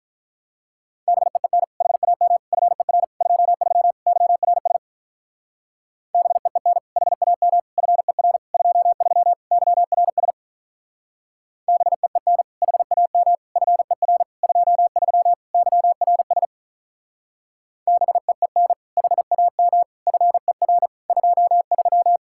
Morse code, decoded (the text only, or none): BEEN HAM FER 23 YRS BEEN HAM FER 23 YRS BEEN HAM FER 23 YRS BEEN HAM FER 23